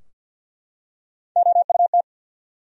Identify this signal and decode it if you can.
Morse code, heard as a keyed tone, about 51 wpm, 700 Hz. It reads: OUT
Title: Top 100 Words Repeated 50wpm